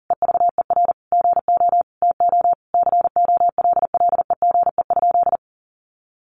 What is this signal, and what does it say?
Morse code, heard as a keyed tone, about 40 wpm, 700 Hz. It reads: EVER GO TO COLLEGE?
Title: Sentences from Top 1300 Words 40wpm